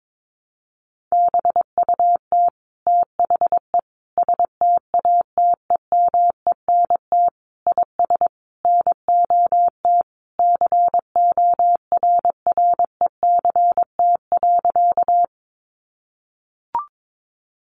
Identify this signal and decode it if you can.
Morse code, heard as a keyed tone, about 22 wpm, 700 Hz. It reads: BUT THE STATEMENT IS NOT CORRECT.
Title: Sentences from Top 1200 Words 22wpm